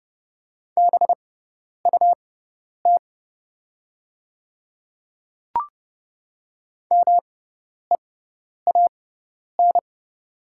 Morse code, heard as a keyed tone, about 30 wpm, spaced as for 10 wpm, 700 Hz. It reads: BUT MEAN
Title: 6x Character Spacing - Top 300 Words - 30wpm